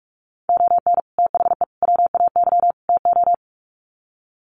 Morse code, heard as a keyed tone, about 44 wpm, 700 Hz. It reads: ON THE WAY TO
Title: Sets of 4 Words 45wpm